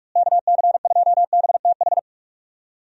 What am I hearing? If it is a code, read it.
Morse code, heard as a keyed tone, about 45 wpm, 700 Hz. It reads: KC2BTH
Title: USA Call Signs - 6 Characters - Repeated 2x - 45wpm